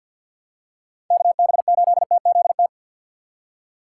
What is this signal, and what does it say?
Morse code, heard as a keyed tone, about 52 wpm, 700 Hz. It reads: KB8TZT
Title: USA Call Signs - 4x Character Spacing - 50wpm